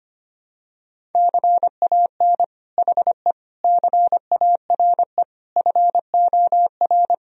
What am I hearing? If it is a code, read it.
Morse code, heard as a keyed tone, about 25 wpm, 700 Hz. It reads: CAN HE CARE FOR
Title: Sentences from Top 1000 Words - Review Part 1 - 25wpm